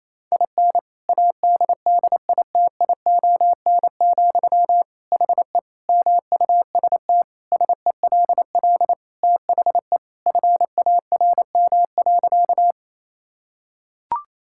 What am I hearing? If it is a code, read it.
Morse code, heard as a keyed tone, about 28 wpm, 700 Hz. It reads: IN ADDITION, HE MUST SELL THE FARM.
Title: Sentences from Top 1000 Words - Review Part 6 - 28wpm